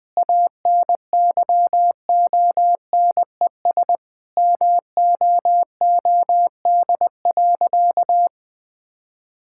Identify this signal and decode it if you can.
Morse code, heard as a keyed tone, about 20 wpm, 700 Hz. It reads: ANYONES MOOD.
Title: Difficult Sentences with Hints - 20wpm